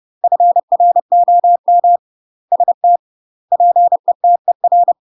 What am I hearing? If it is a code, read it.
Morse code, heard as a keyed tone, about 30 wpm, 700 Hz. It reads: FROM ST PETER
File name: UK MASTER QSOs TRAINING / 30WPM / UK MASTER QSOs 0047